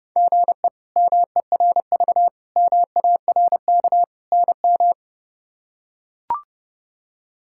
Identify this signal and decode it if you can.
Morse code, heard as a keyed tone, about 30 wpm, 700 Hz. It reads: GE MERV MARK NM
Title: K1USN SST QSO Practice - 30wpm